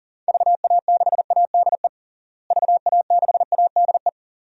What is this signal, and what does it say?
Morse code, heard as a keyed tone, about 40 wpm, 700 Hz. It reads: VA6ADE VA6ADE
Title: DXCC-100 Call Signs - 6 Characters - Repeated 2x - 40wpm